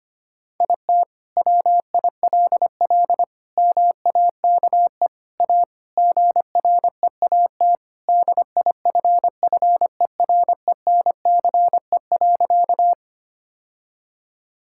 Morse code, heard as a keyed tone, about 25 wpm, 700 Hz. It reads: IT WILL MAKE A GREAT DIFFERENCE.
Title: Sentences from Top 1000 Words - Review Part 5 - 25wpm